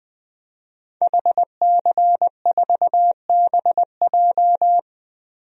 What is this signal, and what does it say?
Morse code, heard as a keyed tone, about 20 wpm, 700 Hz. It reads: HC4BJ
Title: DXCC-100 Call Signs - 5 Characters - Repeated 2x - 20wpm